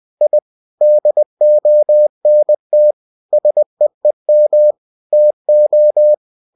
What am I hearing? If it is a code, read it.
Morse code, heard as a keyed tone, about 20 wpm, 600 Hz. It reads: I DONT SEEM TO
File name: cw-books / winnie-the-pooh / winnie-20wpm-page-0026